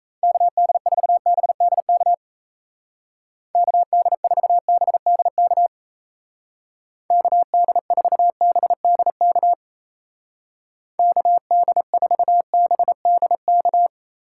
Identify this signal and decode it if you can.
Morse code, heard as a keyed tone, about 33 wpm, 700 Hz. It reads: KD4BDK KD4BDK KD4BDK KD4BDK
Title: Speed Racing - USA 2x3 Call Signs 28wpm